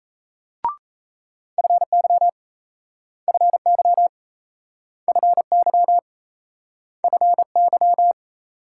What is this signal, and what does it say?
Morse code, heard as a keyed tone, about 35 wpm, 700 Hz. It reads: FY FY FY FY